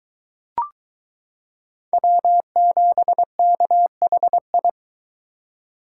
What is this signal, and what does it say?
Morse code, heard as a keyed tone, about 23 wpm, 700 Hz. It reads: W7KHI